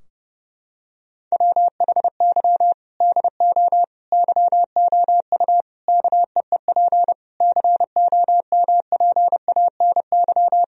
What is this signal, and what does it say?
Morse code, heard as a keyed tone, about 32 wpm, 700 Hz. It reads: WHY DO YOU KEEP COMPANY
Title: Sentences from Top 1000 Words - Review Part 3 - 30wpm